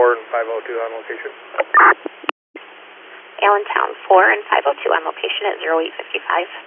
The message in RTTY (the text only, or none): none